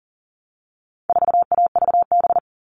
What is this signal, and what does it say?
Morse code, heard as a keyed tone, about 40 wpm, 700 Hz. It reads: 4AVB
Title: All Cut Numbers - 4-Digit 40wpm